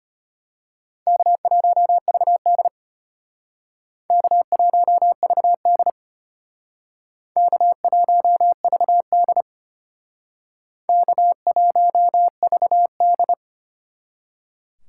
Morse code, heard as a keyed tone, about 32 wpm, 700 Hz. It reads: K1VD K1VD K1VD K1VD